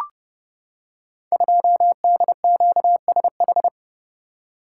Morse code, heard as a keyed tone, about 30 wpm, 700 Hz. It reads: 2DQSH
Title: Code Groups - 5 Letter:Numbers - 30wpm